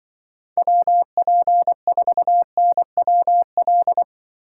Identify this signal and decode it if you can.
Morse code, heard as a keyed tone, about 24 wpm, 700 Hz. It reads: WP4NWL
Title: Speed Racing - USA 2x3 Call Signs 20wpm